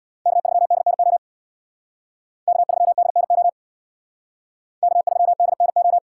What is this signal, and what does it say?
Morse code, heard as a keyed tone, about 63 wpm, 700 Hz. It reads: K3DNO K3DNO K3DNO